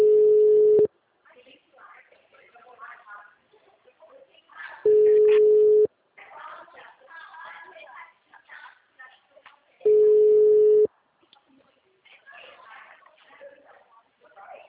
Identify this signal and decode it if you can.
Ringback tone (Europe, 425 Hz), with speech.